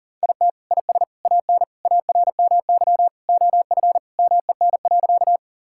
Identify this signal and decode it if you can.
Morse code, heard as a keyed tone, about 40 wpm, 700 Hz. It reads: IT IS AN ARMY OF MEN.
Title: Sentences from Top 1000 Words - Review Part 5 - 40wpm